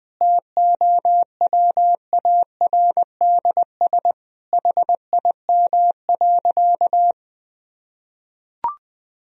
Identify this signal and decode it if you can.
Morse code, heard as a keyed tone, about 20 wpm, 700 Hz. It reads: TOWARDS HIM.
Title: Sentences from Top 1000 Words - Review Part 2 - 20wpm